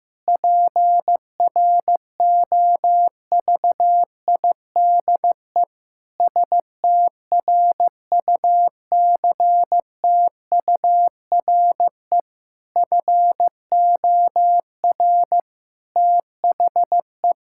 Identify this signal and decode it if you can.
Morse code, heard as a keyed tone, about 16 wpm, 700 Hz. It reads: PROVIDE STRUCTURE FOR THE
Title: Sentences from Top 1000 Words - Review Part 6 - 15wpm